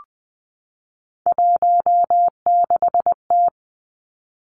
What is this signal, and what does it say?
Morse code, heard as a keyed tone, about 20 wpm, 700 Hz. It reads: 16T
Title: Three Challenging Letter-Number Combinations 20wpm